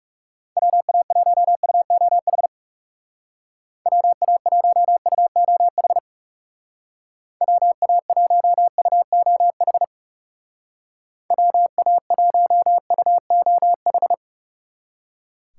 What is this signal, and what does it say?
Morse code, heard as a keyed tone, about 38 wpm, 700 Hz. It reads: WA1UOH WA1UOH WA1UOH WA1UOH